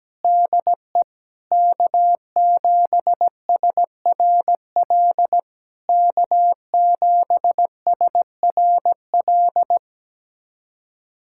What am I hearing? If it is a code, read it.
Morse code, heard as a keyed tone, about 17 wpm, 690 Hz. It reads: DE K7SRL K7SRL
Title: SOTA QSO Practice 17wpm